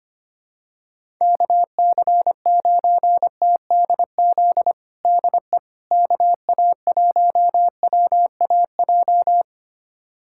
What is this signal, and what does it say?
Morse code, heard as a keyed tone, about 25 wpm, 700 Hz. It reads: KC9TDZ DE KA1WAJ
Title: USA Callsign DE Callsign - 6 Characters - 25wpm